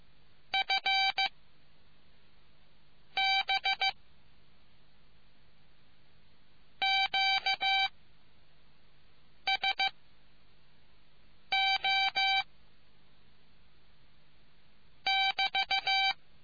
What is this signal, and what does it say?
Morse code, heard as a keyed tone, about 15 wpm, spaced as for 4 wpm, 2350 Hz. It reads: FB QSO <BT>